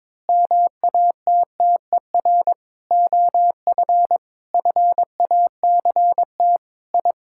Morse code, heard as a keyed tone, about 22 wpm, 700 Hz. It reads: MATTER OF FACT I
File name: practice / 22wpm / Sentences with 4-Word Phrases - 22wpm